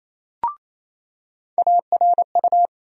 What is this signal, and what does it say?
Morse code, heard as a keyed tone, about 28 wpm, 700 Hz. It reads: ARU
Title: USA Call Sign Postfixes - 3 Letters 28wpm